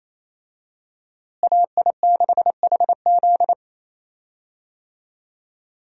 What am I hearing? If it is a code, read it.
Morse code, heard as a keyed tone, about 28 wpm, 700 Hz. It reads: AI6HZ
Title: USA Call Signs - 5 Characters - Repeated 3x - 28wpm